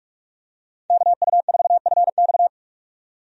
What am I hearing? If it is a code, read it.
Morse code, heard as a keyed tone, about 41 wpm, 700 Hz. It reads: KU4FX
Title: USA Call Signs - 5 Characters - 45wpm